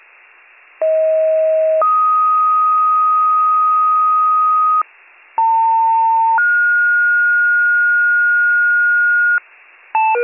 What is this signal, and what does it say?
Two-tone page, heard as a one-second tone, then a three-second tone, about 640 then 1250 Hz, about 900 then 1430 Hz, with noise. It is heard from 0.8 to 9.4 s.